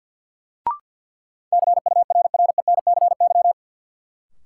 Morse code, heard as a keyed tone, about 47 wpm, 700 Hz. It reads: CURRENCY
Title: Sentences from Top 1500 Words - New Word Drill Alt - 50wpm